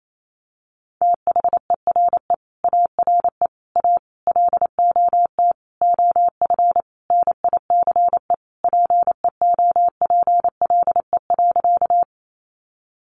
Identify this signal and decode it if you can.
Morse code, heard as a keyed tone, about 28 wpm, 700 Hz. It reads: THERE ARE A LOT OF NICE PEOPLE.
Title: Sentences from Top 1100 Words 28wpm